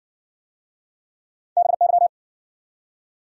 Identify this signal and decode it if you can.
Morse code, heard as a keyed tone, about 60 wpm, 700 Hz. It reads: BY